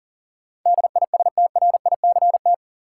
Morse code, heard as a keyed tone, about 40 wpm, 700 Hz. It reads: DISTRICT
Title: Sentences from Top 1600 Words - New Word Drill - 40wpm